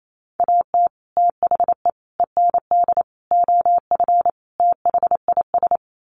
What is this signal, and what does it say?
Morse code, heard as a keyed tone, about 28 wpm, 700 Hz. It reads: AT THE END OF THIS
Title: Sets of 5 Words 28wpm